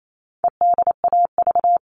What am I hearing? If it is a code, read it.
Morse code, heard as a keyed tone, about 28 wpm, 700 Hz. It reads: EDAV
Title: All Cut Numbers - 4-Digit 28wpm